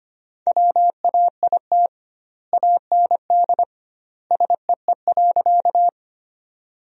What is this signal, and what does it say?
Morse code, heard as a keyed tone, about 25 wpm, 700 Hz. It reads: WAIT AND SEE.
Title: 2x Word-Spacing - Sentences from Top 500 Words - 25wpm